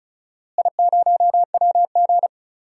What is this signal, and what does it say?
Morse code, heard as a keyed tone, about 34 wpm, 700 Hz. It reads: I0WG
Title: DXCC-100 Call Signs - 4 Characters - Repeated 2x - 35wpm